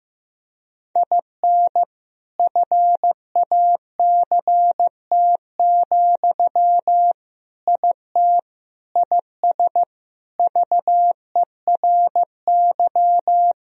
Morse code, heard as a keyed tone, about 15 wpm, 700 Hz. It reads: IN FACT, IT IS VERY
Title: Sentences from Top 1000 Words - Review Part 3 - 15wpm